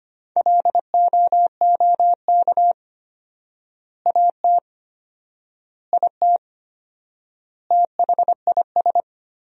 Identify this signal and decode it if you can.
Morse code, heard as a keyed tone, about 25 wpm, 700 Hz. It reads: LOOK AT IT THIS
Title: 4x Word-Spacing - Sentences from Top 100 Words - 25wpm